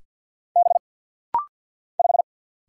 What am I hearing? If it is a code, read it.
Morse code, heard as a keyed tone, about 50 wpm, 700 Hz. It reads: B 5
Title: Single Letter-Number Alt - Rapid-Fire 50wpm